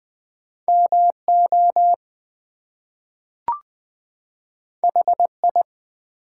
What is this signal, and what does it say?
Morse code, heard as a keyed tone, about 20 wpm, 700 Hz. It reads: MO HI
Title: US States and Canadian Provinces 20wpm